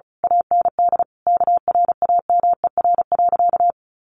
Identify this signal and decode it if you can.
Morse code, heard as a keyed tone, about 34 wpm, 700 Hz. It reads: AND KRAMER.
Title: A Sound Of Thunder 35wpm